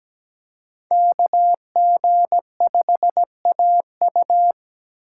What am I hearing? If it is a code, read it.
Morse code, heard as a keyed tone, about 17 wpm, 700 Hz. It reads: KG5AU
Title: USA Call Signs - 6x Character Spacing - 17wpm